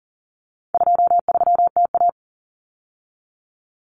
Morse code, heard as a keyed tone, about 40 wpm, 700 Hz. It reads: 23TA
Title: Common Cut Numbers - 4-Digit 40wpm